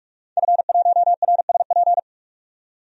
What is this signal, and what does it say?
Morse code, heard as a keyed tone, about 43 wpm, 700 Hz. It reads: F1RSP
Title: DXCC-100 Call Signs - 5 Characters - 45wpm